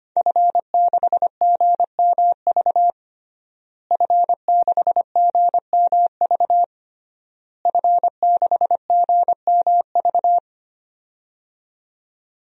Morse code, heard as a keyed tone, about 25 wpm, 700 Hz. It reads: F6GMV F6GMV F6GMV
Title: DXCC-100 Call Signs - 5 Characters - Repeated 3x - 25wpm